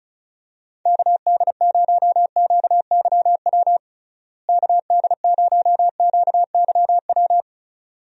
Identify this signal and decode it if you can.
Morse code, heard as a keyed tone, about 36 wpm, 700 Hz. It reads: KD0QYW KD0QYW
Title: USA Call Signs - 6 Characters - Repeated 2x - 35wpm